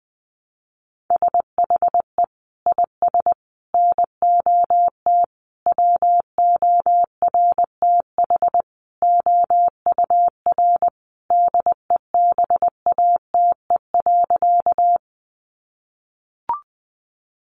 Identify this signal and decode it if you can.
Morse code, heard as a keyed tone, about 20 wpm, 700 Hz. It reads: SHE IS NOT WORTH OUR DEBATE.
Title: Sentences from Top 1100 Words 20wpm